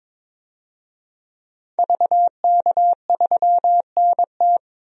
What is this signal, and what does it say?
Morse code, heard as a keyed tone, about 22 wpm, 700 Hz. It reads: VK3NT
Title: DXCC-100 Call Signs - 5 Characters - 22wpm